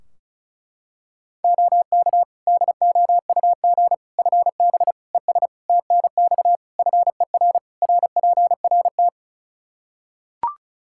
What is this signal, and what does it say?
Morse code, heard as a keyed tone, about 36 wpm, 700 Hz. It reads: OK DOUG FB ES TNX FER RPRT
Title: K1USN SST QSO Practice 35wpm